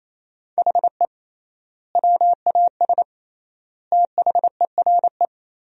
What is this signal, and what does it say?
Morse code, heard as a keyed tone, about 28 wpm, 700 Hz. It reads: HE WAS THERE
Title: 3x Word-Spacing - Sentences from Top 200 Words - 28wpm